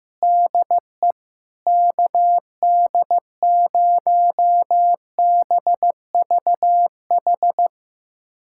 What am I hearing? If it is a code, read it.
Morse code, heard as a keyed tone, about 15 wpm, 700 Hz. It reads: DE KD0BVH
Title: USA Callsign DE Callsign - 6 Characters - 15wpm